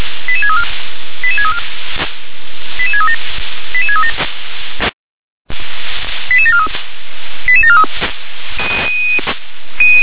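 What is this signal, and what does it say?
Five-tone selective call (ZVEI), heard as quick runs of five tones, tones of about 70 ms. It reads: 88634 88634 88637 88637 88743 88743
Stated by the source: ZVEI five-tone call 88631 received by SDR.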